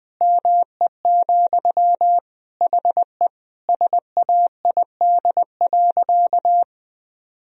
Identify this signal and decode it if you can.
Morse code, heard as a keyed tone, about 20 wpm, 700 Hz. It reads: ME, HE SAID.
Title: Sentences from Top 1000 Words - Review Part 1 - 20wpm